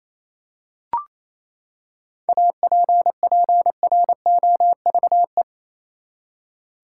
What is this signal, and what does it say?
Morse code, heard as a keyed tone, about 28 wpm, 700 Hz. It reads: APPROVE
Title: Seven Letter Words - 28wpm